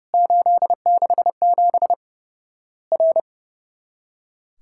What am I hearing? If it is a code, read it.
Morse code, heard as a keyed tone, about 29 wpm, 700 Hz. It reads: 867 R
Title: Worked All Europe DX Contest - QTC Exchange - 30wpm